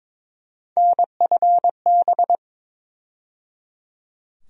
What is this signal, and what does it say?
Morse code, heard as a keyed tone, about 22 wpm, 700 Hz. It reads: NFB
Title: USA Call Sign Postfixes - 3 Letters 22wpm